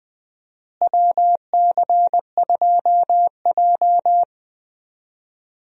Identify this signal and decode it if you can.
Morse code, heard as a keyed tone, about 20 wpm, 700 Hz. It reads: WC2J
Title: USA Call Signs - 4 Characters - Repeated 3x - 20wpm